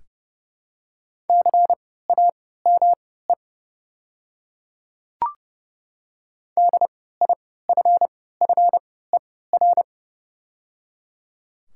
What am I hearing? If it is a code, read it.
Morse code, heard as a keyed tone, about 30 wpm, spaced as for 17 wpm, 700 Hz. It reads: CAME DIFFER
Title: 3x Character Spacing - Top 200 Words - 30wpm